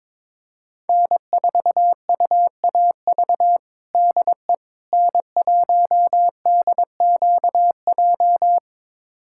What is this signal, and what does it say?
Morse code, heard as a keyed tone, about 22 wpm, 700 Hz. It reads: N4UAV DE N1DQJ
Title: USA Callsign DE Callsign - 5 Characters - 22wpm